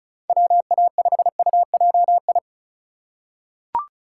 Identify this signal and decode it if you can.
Morse code, heard as a keyed tone, about 36 wpm, 690 Hz. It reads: WA5UJI
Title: USA Call Signs - 6 Characters - 4x Character Spacing - 35wpm